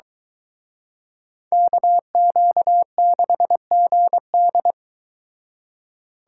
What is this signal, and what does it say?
Morse code, heard as a keyed tone, about 23 wpm, 700 Hz. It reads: KQ6GD